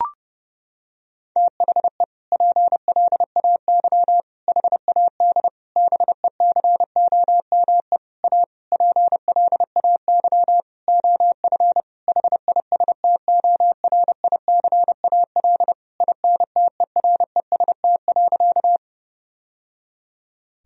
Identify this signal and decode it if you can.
Morse code, heard as a keyed tone, about 30 wpm, 700 Hz. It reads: THE PLAY HAD BECOME A PLAY OF HISTORICAL INTEREST.
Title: Sentences from Top 1000 Words - Review Part 4 - 30wpm